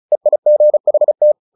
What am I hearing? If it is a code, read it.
Morse code, heard as a keyed tone, about 34 wpm, 600 Hz. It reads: EIGHT
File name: the old man and the sea 2 35wpm